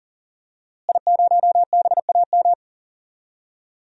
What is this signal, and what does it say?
Morse code, heard as a keyed tone, about 40 wpm, 700 Hz. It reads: I0BAM
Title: DXCC-100 Call Signs - 5 Characters - Repeated 3x - 40wpm